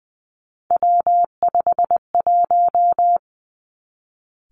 Morse code, heard as a keyed tone, about 20 wpm, 700 Hz. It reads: W51
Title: Three Challenging Letter-Number Combinations 20wpm